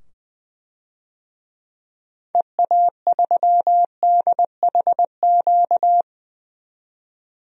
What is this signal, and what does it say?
Morse code, heard as a keyed tone, about 20 wpm, 700 Hz. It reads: EA3DHQ
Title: DXCC-100 Call Signs - 6 Characters - 20wpm